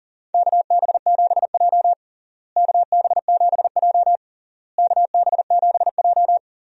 Morse code, heard as a keyed tone, about 40 wpm, 700 Hz. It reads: KB7J KB7J KB7J
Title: USA Call Signs - 4 Characters - Repeated 3x - 40wpm